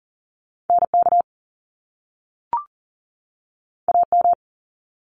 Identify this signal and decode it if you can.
Morse code, heard as a keyed tone, about 40 wpm, 700 Hz. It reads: NK AM
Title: Top Two Letter Combinations in Words 40wpm